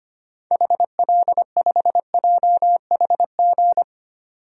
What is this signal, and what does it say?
Morse code, heard as a keyed tone, about 25 wpm, 700 Hz. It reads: HL5JHG
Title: DXCC-100 Call Signs - 6 Characters - 25wpm